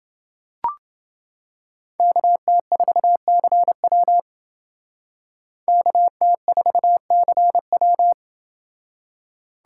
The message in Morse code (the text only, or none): KT4CW KT4CW